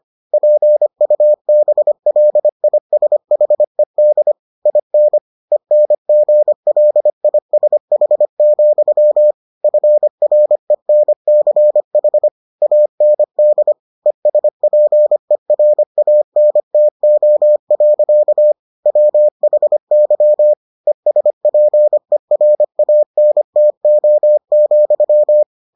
Morse code, heard as a keyed tone, about 25 wpm, 600 Hz. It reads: PUBLISHED IN ENGLISH, FRENCH AND ESPERANTO. WHY ESPERANTO,